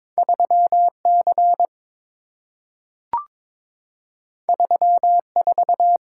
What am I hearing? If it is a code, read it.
Morse code, heard as a keyed tone, about 23 wpm, 700 Hz. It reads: 3C 34
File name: practice / 22wpm / Two Letter-Number Combinations - 22wpm